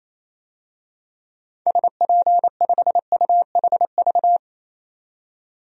Morse code, heard as a keyed tone, about 28 wpm, 700 Hz. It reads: SP5UHV
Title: DXCC-100 Call Signs - 6 Characters - 28wpm